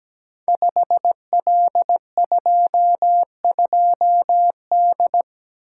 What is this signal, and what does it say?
Morse code, heard as a keyed tone, about 17 wpm, 700 Hz. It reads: 5L22D
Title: Code Groups - 5 Letter:Numbers - 17wpm